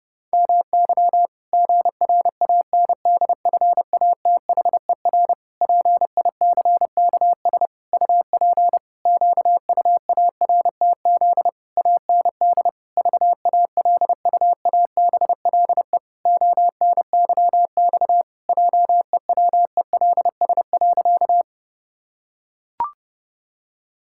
Morse code, heard as a keyed tone, about 30 wpm, 700 Hz. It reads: MY GRANDFATHER PICKS UP QUARTZ AND VALUABLE ONYX JEWELS.
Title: Pangram Challenge v2 - 30wpm